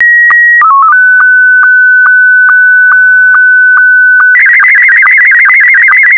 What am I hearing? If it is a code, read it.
SSTV (slow-scan television), Scottie 1.
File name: SSTV Scottie1 Leo